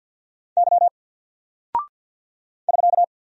Morse code, heard as a keyed tone, about 48 wpm, 700 Hz. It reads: Y <SK>
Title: Single Character (Letter:Number:Punctuation:Prosign) Alt 50wpm